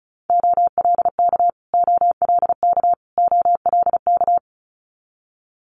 Morse code, heard as a keyed tone, about 36 wpm, 700 Hz. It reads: OLK OLK OLK
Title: Multi-letter Phonemes Repeated 3x 35wpm